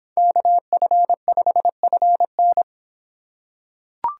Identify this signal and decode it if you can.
Morse code, heard as a keyed tone, about 26 wpm, 700 Hz. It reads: KF5FN